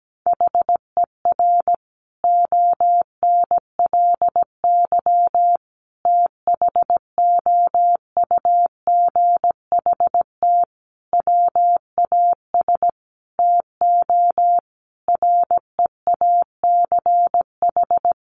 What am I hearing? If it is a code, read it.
Morse code, heard as a keyed tone, about 17 wpm, 700 Hz. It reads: HER ONLY THOUGHT WAS TO REACH
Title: Sentences from Top 400 Words 17wpm